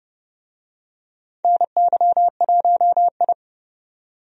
Morse code, heard as a keyed tone, about 30 wpm, 700 Hz. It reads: NY1I
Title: USA Call Signs - 4 Characters - 30wpm